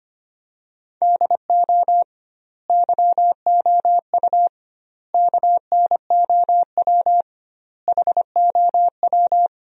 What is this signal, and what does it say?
Morse code, heard as a keyed tone, about 25 wpm, 700 Hz. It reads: DO YOU KNOW HOW
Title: 2x Word-Spacing - Sentences from Top 300 Words - 25wpm